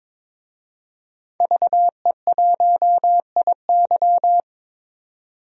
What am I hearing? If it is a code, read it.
Morse code, heard as a keyed tone, about 22 wpm, 700 Hz. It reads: VE1IY
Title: DXCC-100 Call Signs - 5 Characters - 22wpm